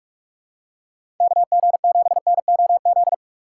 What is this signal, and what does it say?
Morse code, heard as a keyed tone, about 44 wpm, 700 Hz. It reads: KG7NOZ
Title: USA Call Signs - 6 Characters - 6x Character Spacing - 45wpm